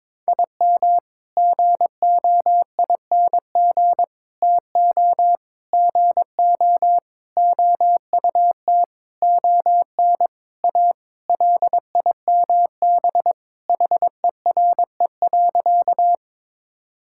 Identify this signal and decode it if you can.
Morse code, heard as a keyed tone, about 22 wpm, 700 Hz. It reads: IM GOING TO GO OUT ON A LIMB HERE.
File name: practice / 22wpm / Sentences with 8-Word Phrases - 22wpm